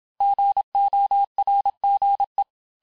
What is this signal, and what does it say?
Morse code, heard as a keyed tone, about 27 wpm, 800 Hz. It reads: GORGE
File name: cours-du-mercredi / audios / cours-869-texte-LES-MOTS-vitesse-25-MPM